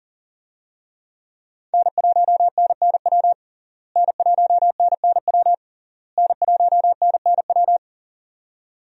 Morse code, heard as a keyed tone, about 40 wpm, 690 Hz. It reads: N1NNW N1NNW N1NNW
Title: USA Call Signs - 5 Characters - Repeated 3x - 40wpm